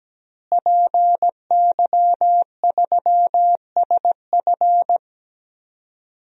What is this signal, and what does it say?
Morse code, heard as a keyed tone, about 17 wpm, 700 Hz. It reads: PY3SF
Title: DXCC-100 Call Signs - 5 Characters - Repeated 2x - 17wpm